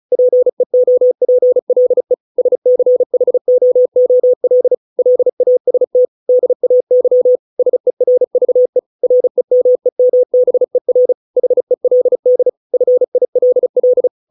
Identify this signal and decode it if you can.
Morse code, heard as a keyed tone, about 34 wpm, 500 Hz. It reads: PEOPLE SCHOOL LAST DAY SERVE REMEMBER HELD FILL